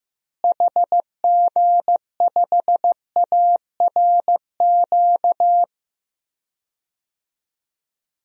Morse code, heard as a keyed tone, about 15 wpm, 700 Hz. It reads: HG5ARQ
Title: DXCC-100 Call Signs - 6 Characters - Repeated 2x - 15wpm